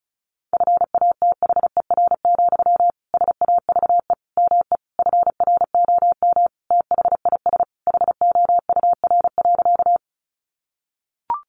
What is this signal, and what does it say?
Morse code, heard as a keyed tone, about 34 wpm, 700 Hz. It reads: FATHER, SAVE ME FROM THIS HOUR.
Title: Sentences from Top 1100 Words 35wpm